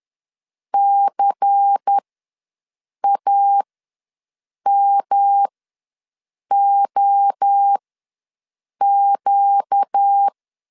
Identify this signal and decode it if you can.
Morse code, heard as a keyed tone, about 10 wpm, spaced as for 6 wpm, 800 Hz. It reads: CAMOQ
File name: LEZIONI / 5 LBVC / 5 LBVC ws 10wpm 3733